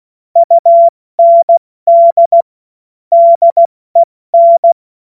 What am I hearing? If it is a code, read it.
Morse code, heard as a keyed tone, about 16 wpm, 680 Hz. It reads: UND DEN
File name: Morsekurs / Witze / Witz 14. Juni 2022 70 F80